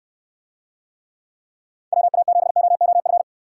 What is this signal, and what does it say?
Morse code, heard as a keyed tone, about 69 wpm, 700 Hz. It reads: WA7OOF